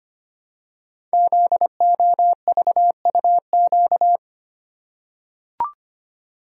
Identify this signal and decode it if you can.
Morse code, heard as a keyed tone, about 25 wpm, 700 Hz. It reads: ZOVUQ
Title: Code Groups - 5 Letters - 25wpm